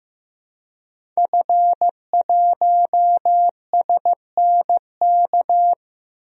Morse code, heard as a keyed tone, about 15 wpm, 700 Hz. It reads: F1SNK
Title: DXCC-100 Call Signs - 5 Characters - 15wpm